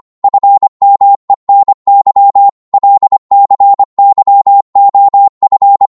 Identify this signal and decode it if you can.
Morse code, heard as a keyed tone, about 25 wpm, 800 Hz. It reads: FMENY LCYOF
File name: Morse / ltr-25wpm